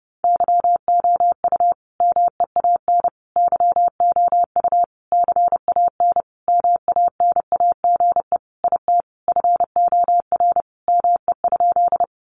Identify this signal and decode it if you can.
Morse code, heard as a keyed tone, about 30 wpm, 700 Hz. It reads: YOU MEAN YOU CAN MANAGE IT FOR ME?
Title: Sentences from Top 1400 Words 30wpm